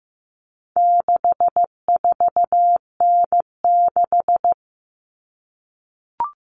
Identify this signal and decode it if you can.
Morse code, heard as a keyed tone, about 15 wpm, 700 Hz. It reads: 64N6
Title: All Cut Numbers - 4-Digit 15wpm